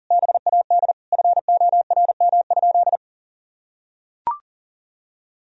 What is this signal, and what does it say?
Morse code, heard as a keyed tone, about 40 wpm, 700 Hz. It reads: BAD FORM?
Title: Sentences from Top 1000 Words - Review Part 1 - 40wpm